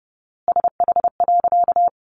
Morse code, heard as a keyed tone, about 30 wpm, 700 Hz. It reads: SH.
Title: A Sound Of Thunder 30wpm